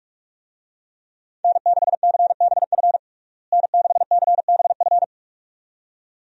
Morse code, heard as a keyed tone, about 46 wpm, 700 Hz. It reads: N6CBF N6CBF
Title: USA Call Signs - 5 Characters - Repeated 2x - 45wpm